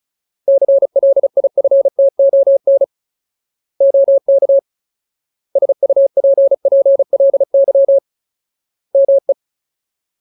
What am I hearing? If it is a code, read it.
Morse code, heard as a keyed tone, about 34 wpm, 550 Hz. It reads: CLIFTON OK SUPPLY ME
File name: Ultimate Word List / 35WPM Ultimate